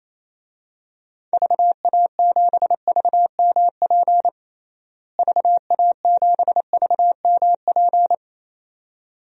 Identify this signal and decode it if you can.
Morse code, heard as a keyed tone, about 28 wpm, 700 Hz. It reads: VA7VMP VA7VMP
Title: DXCC-100 Call Signs - 6 Characters - Repeated 2x - 28wpm